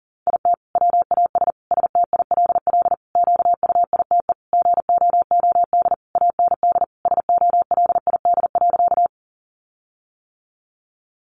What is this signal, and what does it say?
Morse code, heard as a keyed tone, about 40 wpm, 700 Hz. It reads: IT WAS STILL QUITE GOOD AND SOLID.
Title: Sentences from Top 1500 Words 40wpm